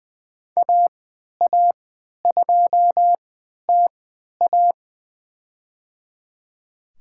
Morse code, heard as a keyed tone, about 20 wpm, spaced as for 11 wpm, 700 Hz. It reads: AA2TA
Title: USA Call Signs - 3x Character Spacing - 20wpm